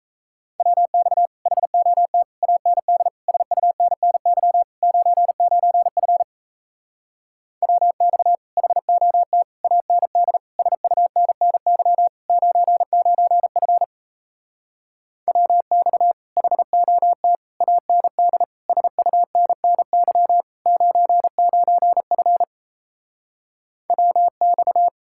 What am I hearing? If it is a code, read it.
Morse code, heard as a keyed tone, about 37 wpm, 700 Hz. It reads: WX HOT AND SUNNY 99F WX HOT AND SUNNY 99F WX HOT AND SUNNY 99F WX